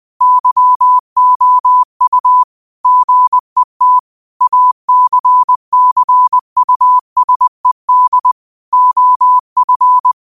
Morse code, heard as a keyed tone, about 20 wpm, 1000 Hz. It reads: YOU GET ACCUSED OF